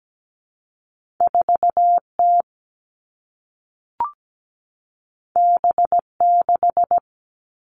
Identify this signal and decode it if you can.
Morse code, heard as a keyed tone, about 17 wpm, 700 Hz. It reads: 4T B6
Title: All Cut Numbers - 2-Digit 17wpm